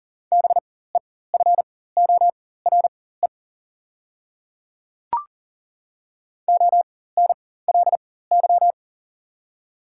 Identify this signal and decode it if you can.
Morse code, heard as a keyed tone, about 40 wpm, spaced as for 19 wpm, 700 Hz. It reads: BEFORE ONLY
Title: 4x Character Spacing - Top 300 Words - 40wpm